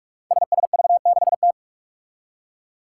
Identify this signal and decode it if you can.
Morse code, heard as a keyed tone, about 42 wpm, 700 Hz. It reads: SSV6T
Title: Five Challenging Letter-Number Combinations 45wpm